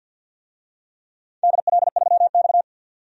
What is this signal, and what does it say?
Morse code, heard as a keyed tone, about 52 wpm, 700 Hz. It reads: DL3X
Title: DXCC-100 Call Signs - 4 Characters - 50wpm